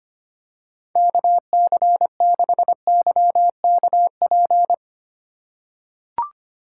Morse code, heard as a keyed tone, about 25 wpm, 700 Hz. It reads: KC6YKP
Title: USA Call Signs - 4x Character Spacing - 25wpm